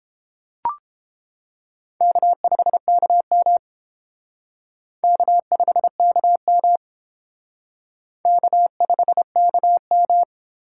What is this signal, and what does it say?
Morse code, heard as a keyed tone, about 28 wpm, 700 Hz. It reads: K5KM K5KM K5KM